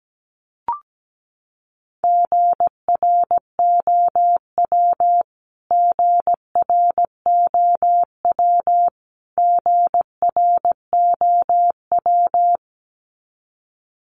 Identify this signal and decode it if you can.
Morse code, heard as a keyed tone, about 17 wpm, 700 Hz. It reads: GROW GROW GROW
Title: Top 300 Words Repeated 17wpm